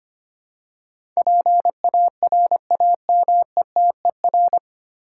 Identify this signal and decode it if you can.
Morse code, heard as a keyed tone, about 25 wpm, 700 Hz. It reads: PARAMETER
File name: Nine Letter Words - 25wpm